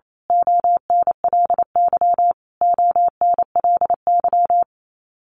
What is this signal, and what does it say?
Morse code, heard as a keyed tone, about 28 wpm, 700 Hz. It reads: ONLY ONLY
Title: Top 300 Words Repeated 28wpm